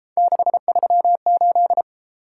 Morse code, heard as a keyed tone, about 33 wpm, 700 Hz. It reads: 638